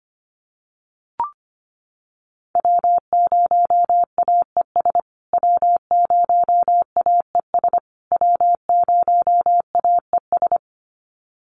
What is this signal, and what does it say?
Morse code, heard as a keyed tone, about 25 wpm, 700 Hz. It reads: W0AES W0AES W0AES
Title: USA Call Signs Repeated 3x 25wpm